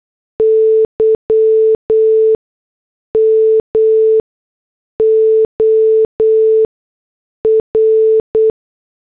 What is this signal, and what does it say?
Morse code, heard as a keyed tone, about 8 wpm, spaced as for 6 wpm, 440 Hz. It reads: Y MOR